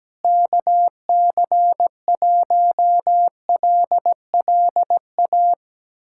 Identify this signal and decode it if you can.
Morse code, heard as a keyed tone, about 17 wpm, 700 Hz. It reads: KC1LLA
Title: USA Callsign DE Callsign 17wpm